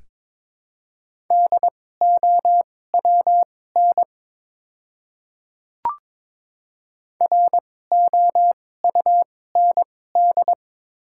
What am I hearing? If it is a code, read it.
Morse code, heard as a keyed tone, about 22 wpm, spaced as for 16 wpm, 700 Hz. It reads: DOWN ROUND
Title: 2x Character Spacing - Top 200 Words - 22wpm